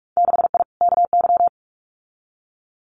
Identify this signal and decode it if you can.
Morse code, heard as a keyed tone, about 44 wpm, 700 Hz. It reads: 6I KY
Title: Winter Field Day Exchanges 45wpm